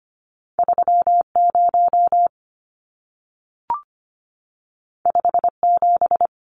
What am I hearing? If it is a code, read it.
Morse code, heard as a keyed tone, about 25 wpm, 700 Hz. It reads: 30 57
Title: Numbers - 2-Digit 25wpm